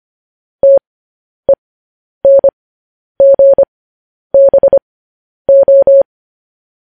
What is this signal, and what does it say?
Morse code, heard as a keyed tone, about 25 wpm, spaced as for 10 wpm, 550 Hz. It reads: TENGBO
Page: news headlines in morse code at 10 wpm